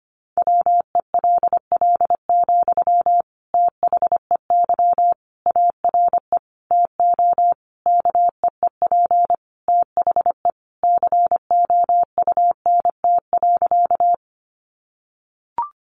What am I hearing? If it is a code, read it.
Morse code, heard as a keyed tone, about 25 wpm, 700 Hz. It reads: WELL, THEY ARE TO KEEP THE COUNT.
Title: Sentences from Top 1500 Words 25wpm